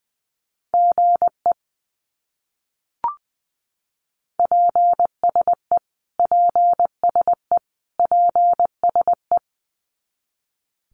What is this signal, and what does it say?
Morse code, heard as a keyed tone, about 20 wpm, 700 Hz. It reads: GE PSE PSE PSE
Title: Abbreviations Repeated 3x - 20wpm